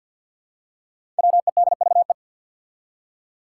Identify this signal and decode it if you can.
Morse code, heard as a keyed tone, about 53 wpm, 700 Hz. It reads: WEDVE
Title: Five Challenging Letter-Number Combinations 50wpm